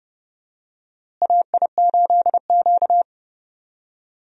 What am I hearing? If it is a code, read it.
Morse code, heard as a keyed tone, about 30 wpm, 700 Hz. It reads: AI8Q
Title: USA Call Signs - 4 Characters - Repeated 3x - 30wpm